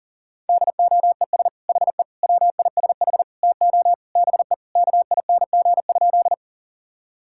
Morse code, heard as a keyed tone, about 40 wpm, 700 Hz. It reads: DOES HE WISH TO BE KING?
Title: Sentences from Top 1600 Words 40wpm